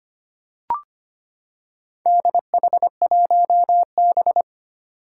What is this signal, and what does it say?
Morse code, heard as a keyed tone, about 25 wpm, 700 Hz. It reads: DH1B
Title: Four Challenging Letter-Number Combinations 25wpm